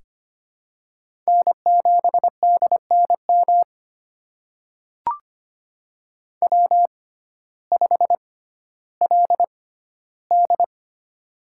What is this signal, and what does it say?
Morse code, heard as a keyed tone, about 25 wpm, spaced as for 14 wpm, 700 Hz. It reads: N7DNM W5LD
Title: USA Call Signs - 6x Character Spacing - 25wpm